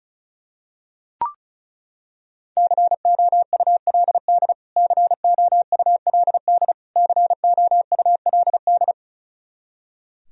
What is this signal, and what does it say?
Morse code, heard as a keyed tone, about 36 wpm, 700 Hz. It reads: COULD COULD COULD
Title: Top 100 Words Repeated 35wpm